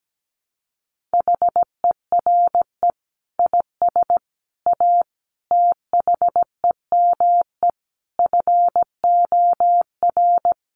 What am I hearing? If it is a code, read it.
Morse code, heard as a keyed tone, about 17 wpm, 700 Hz. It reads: HERE IS A THEME FOR
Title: Sentences from Top 1100 Words 17wpm